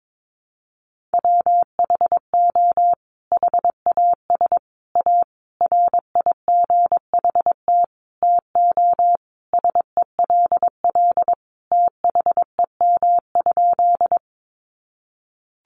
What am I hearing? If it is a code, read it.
Morse code, heard as a keyed tone, about 22 wpm, 700 Hz. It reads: WHO HAS A RIGHT TO SELL THEM?
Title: Sentences from Top 1100 Words 22wpm